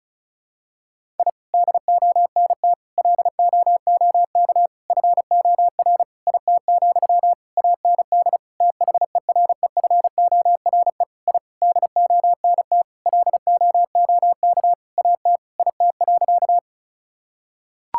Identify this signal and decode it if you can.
Morse code, heard as a keyed tone, about 34 wpm, 700 Hz. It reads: I DONT LOOK FOR IT, AND THEREFORE I DONT LOOK AT IT.
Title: Sentences from Top 1000 Words - Review Part 5 - 35wpm